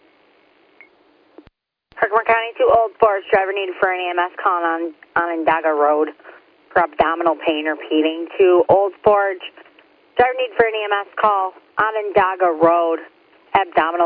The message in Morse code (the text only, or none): none